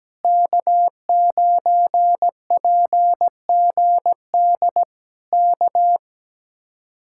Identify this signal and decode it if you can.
Morse code, heard as a keyed tone, about 17 wpm, 700 Hz. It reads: K9PGD K
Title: Speed-Racing Common QSO Phrases - 17wpm